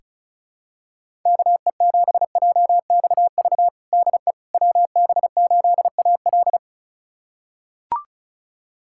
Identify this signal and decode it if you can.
Morse code, heard as a keyed tone, about 34 wpm, 700 Hz. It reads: KE7JXV DE WB8AL
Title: USA Callsign DE Callsign 35wpm